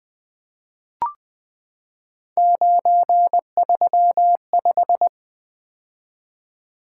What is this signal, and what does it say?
Morse code, heard as a keyed tone, about 20 wpm, 700 Hz. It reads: 935
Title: Three Letter-Number Combinations - 20wpm